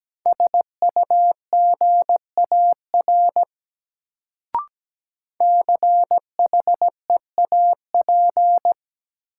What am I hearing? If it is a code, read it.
Morse code, heard as a keyed tone, about 17 wpm, 700 Hz. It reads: SUGAR CHEAP
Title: Sentences from Top 1200 Words - New Word Drill Alt - 17wpm